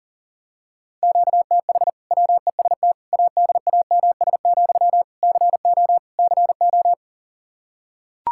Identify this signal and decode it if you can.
Morse code, heard as a keyed tone, about 40 wpm, 700 Hz. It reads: QTH WEST ADAMS, CO CO
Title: Practice QSO - Exchange 1 - Beginning - 40wpm